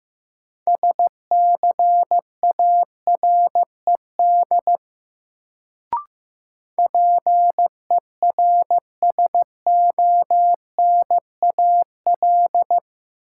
Morse code, heard as a keyed tone, about 15 wpm, 700 Hz. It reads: SCARED PERSONAL
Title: Sentences from Top 700 Words - New Word Drill - 15wpm